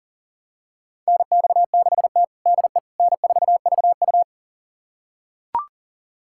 Morse code, heard as a keyed tone, about 40 wpm, 700 Hz. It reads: NX6T DE N4VU
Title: USA Callsign DE Callsign - 4 Characters - 40wpm